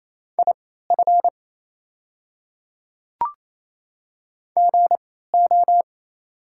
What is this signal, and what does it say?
Morse code, heard as a keyed tone, about 28 wpm, spaced as for 16 wpm, 690 Hz. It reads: IF GO
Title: 3x Character Spacing - Top 100 Words - 28wpm